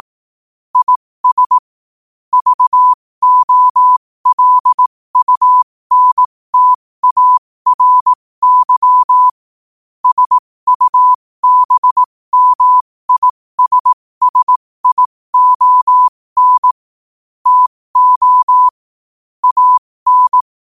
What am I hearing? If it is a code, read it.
Morse code, heard as a keyed tone, about 18 wpm, spaced as for 15 wpm, 1000 Hz. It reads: IS VOLUNTARY SUBMISSION TO AN
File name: qotdfiles / 15wpm